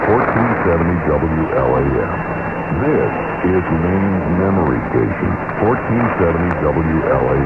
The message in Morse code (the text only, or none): none